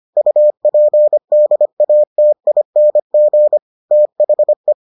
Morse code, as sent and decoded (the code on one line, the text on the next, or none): ..- .--. -.. .- - .. -. --. / - .... .
UPDATING THE